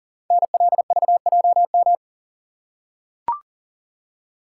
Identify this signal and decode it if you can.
Morse code, heard as a keyed tone, about 40 wpm, 700 Hz. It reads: NLVJM
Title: Code Groups - 5 Letters - 40wpm